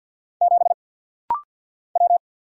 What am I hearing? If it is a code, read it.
Morse code, heard as a keyed tone, about 49 wpm, 700 Hz. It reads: 7 W
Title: Single Letter-Number - Rapid-Fire 50wpm